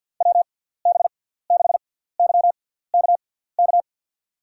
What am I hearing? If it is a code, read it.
Morse code, heard as a keyed tone, about 53 wpm, spaced as for 17 wpm, 700 Hz. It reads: WB6YKK
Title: USA Call Signs - 6 Characters - 6x Character Spacing - 50wpm